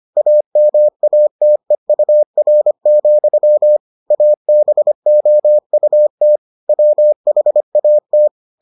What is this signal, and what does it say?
Morse code, heard as a keyed tone, about 25 wpm, 600 Hz. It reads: AMATEUR, ABOUT WHAT